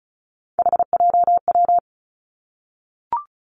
Morse code, heard as a keyed tone, about 36 wpm, 700 Hz. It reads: HJW
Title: Three Challenging Letter-Number Combinations 35wpm